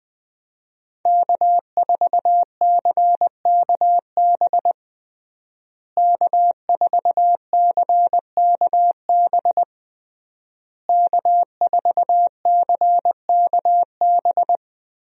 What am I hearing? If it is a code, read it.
Morse code, heard as a keyed tone, about 20 wpm, 700 Hz. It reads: K4CKB K4CKB K4CKB
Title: USA Call Signs - 5 Characters - Repeated 3x - 20wpm